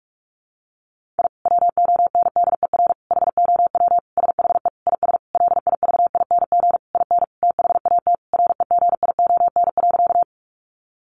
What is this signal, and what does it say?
Morse code, heard as a keyed tone, about 44 wpm, 700 Hz. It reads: I WONDER HOW SHE IS LIVING IN THAT REGION.
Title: Sentences from Top 1500 Words 45wpm